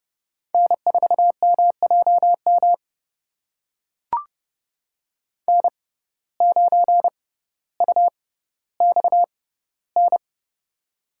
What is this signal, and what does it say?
Morse code, heard as a keyed tone, about 31 wpm, spaced as for 16 wpm, 700 Hz. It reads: N4MJM N9UXN
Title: USA Call Signs - 6x Character Spacing - 30wpm